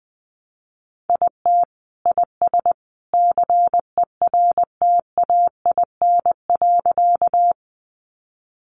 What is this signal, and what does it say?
Morse code, heard as a keyed tone, about 20 wpm, 700 Hz. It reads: IT IS CERTAIN.
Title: Sentences from Top 500 Words 20wpm